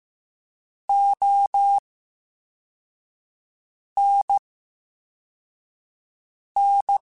Morse code, heard as a keyed tone, about 15 wpm, spaced as for 4 wpm, 800 Hz. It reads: ONN
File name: Special INGs / Special INGs-mp3s / Slow ing @13